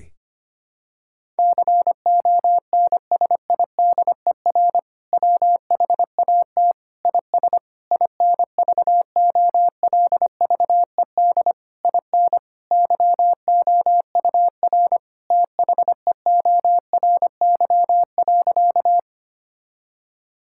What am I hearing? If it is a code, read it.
Morse code, heard as a keyed tone, about 25 wpm, 700 Hz. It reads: CONSIDER WHAT IS INVOLVED IN YOUR THEORY.
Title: Sentences from Top 1000 Words - Review Part 5 - 25wpm